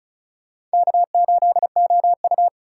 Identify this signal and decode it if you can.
Morse code, heard as a keyed tone, about 34 wpm, 700 Hz. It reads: K8OU
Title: USA Call Signs - 4 Characters - 4x Character Spacing - 35wpm